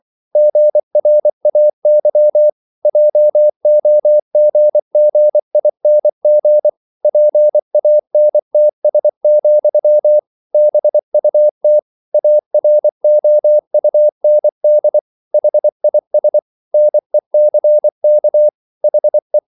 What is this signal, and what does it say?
Morse code, heard as a keyed tone, about 24 wpm, 600 Hz. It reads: GRAY JOGGING PANTS, BUT AROUND HIS NECK HE